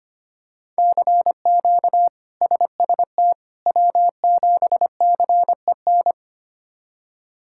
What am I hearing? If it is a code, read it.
Morse code, heard as a keyed tone, about 25 wpm, 700 Hz. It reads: CQ SST W7CEN
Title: K1USN SST QSO Practice - 25wpm